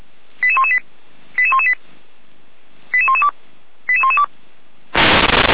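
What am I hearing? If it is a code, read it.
Five-tone selective call (ZVEI), heard as quick runs of five tones, tones of about 70 ms. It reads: 88118 88118 88112 88112
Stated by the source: ZVEI five-tone selective call received on an SDR.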